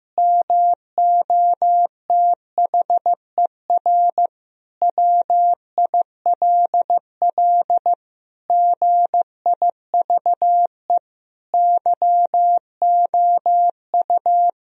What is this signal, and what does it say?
Morse code, heard as a keyed tone, about 15 wpm, 700 Hz. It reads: MOTHER WILL GIVE YOU
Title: Sentences from Top 1000 Words - Review Part 1 - 15wpm